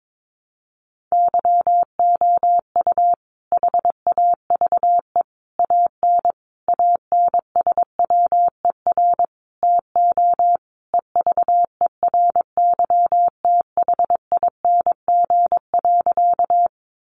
Morse code, heard as a keyed tone, about 22 wpm, 700 Hz. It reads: YOU HAVE AN ANSWER TO EVERYTHING.
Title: Sentences from Top 700 Words 22wpm